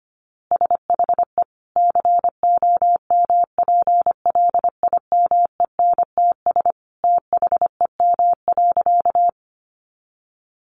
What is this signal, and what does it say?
Morse code, heard as a keyed tone, about 25 wpm, 700 Hz. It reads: SHE COMPLIMENTS THEM.
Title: Easy Phrases 25wpm